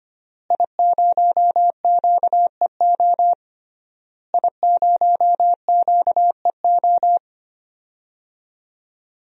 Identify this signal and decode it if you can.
Morse code, heard as a keyed tone, about 25 wpm, 700 Hz. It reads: I0QEO I0QEO
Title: DXCC-100 Call Signs - 5 Characters - Repeated 3x - 25wpm